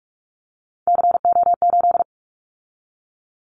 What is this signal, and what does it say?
Morse code, heard as a keyed tone, about 44 wpm, 700 Hz. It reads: CO8
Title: Callsign Prefixes for Top 100 DXCC - 45wpm